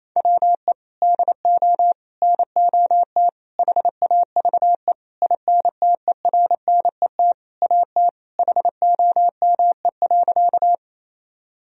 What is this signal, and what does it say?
Morse code, heard as a keyed tone, about 28 wpm, 700 Hz. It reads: WE DO NOT HAVE INTERNET AT HOME.
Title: Sentences from Top 1000 Words - Review Part 4 - 28wpm